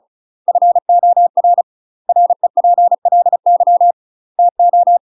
Morse code, heard as a keyed tone, about 35 wpm, 700 Hz. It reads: FOR REPLY TO
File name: UK MASTER QSOs TRAINING / 35WPM / UK MASTER QSOs 0020